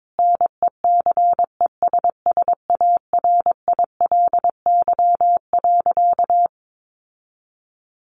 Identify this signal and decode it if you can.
Morse code, heard as a keyed tone, about 22 wpm, 700 Hz. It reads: NECESSARILY.
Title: Sentences from Top 1400 Words 22wpm